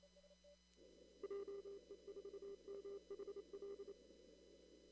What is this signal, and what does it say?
Morse code, heard as a keyed tone, about 27 wpm, 400 Hz. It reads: JE4MHL
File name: CW / kt5j wpx cw-op-k5pi